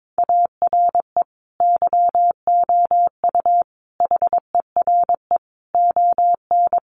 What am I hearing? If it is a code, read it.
Morse code, heard as a keyed tone, about 22 wpm, 700 Hz. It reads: ARE YOU HERE ON
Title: Sentences from Top 400 Words 22wpm